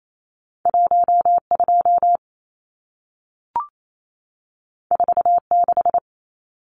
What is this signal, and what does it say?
Morse code, heard as a keyed tone, about 28 wpm, 700 Hz. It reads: 12 46
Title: Numbers - 2-Digit 28wpm